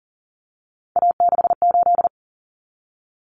Morse code, heard as a keyed tone, about 40 wpm, 700 Hz. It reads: A68
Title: Common Cut Numbers - 3-Digit 40wpm